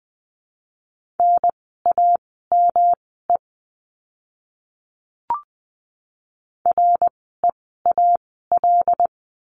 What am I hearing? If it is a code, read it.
Morse code, heard as a keyed tone, about 20 wpm, spaced as for 14 wpm, 700 Hz. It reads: NAME REAL
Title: 108 Lesson 18f - Letter m - Words 20wpm